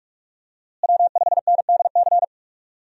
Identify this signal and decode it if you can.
Morse code, heard as a keyed tone, about 46 wpm, 700 Hz. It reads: W5NDC
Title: USA Call Signs - 5 Characters - Repeated 3x - 45wpm